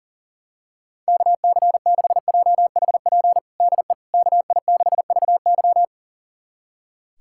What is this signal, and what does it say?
Morse code, heard as a keyed tone, about 40 wpm, 700 Hz. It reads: KC6JHP DE KI6VY
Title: USA Callsign DE Callsign 40wpm